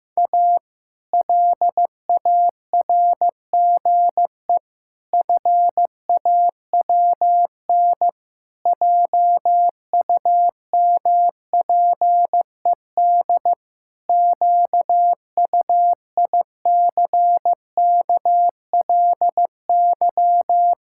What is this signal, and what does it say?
Morse code, heard as a keyed tone, about 15 wpm, 700 Hz. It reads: A LARGE FAWN JUMPED QUICKLY
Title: Pangram Challenge v2 - 15wpm